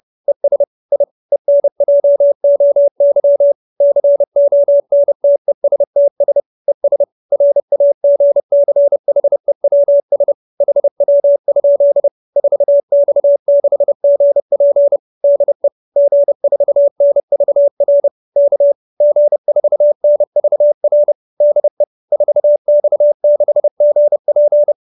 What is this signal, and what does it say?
Morse code, heard as a keyed tone, about 30 wpm, 570 Hz. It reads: ES I ENJOY CONTESTS ES RAGCHEWS HW? 4X6GP DE G4NVR K G4NVR DE 4X6GP
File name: QSO315 30